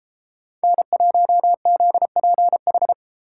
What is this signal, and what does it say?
Morse code, heard as a keyed tone, about 32 wpm, 700 Hz. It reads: N1ZPH